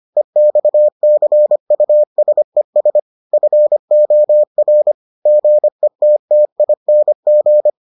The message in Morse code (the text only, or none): EXCUSES FOR GETTING